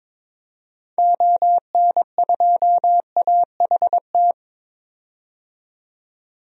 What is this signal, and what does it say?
Morse code, heard as a keyed tone, about 22 wpm, 700 Hz. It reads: ON2AHT
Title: DXCC-100 Call Signs - 6 Characters - 22wpm